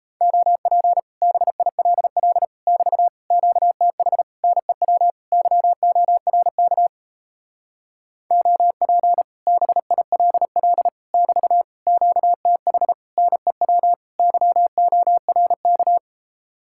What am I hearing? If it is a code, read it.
Morse code, heard as a keyed tone, about 37 wpm, 700 Hz. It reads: OP BILL <BT> QTH NEW YORK OP BILL <BT> QTH NEW YORK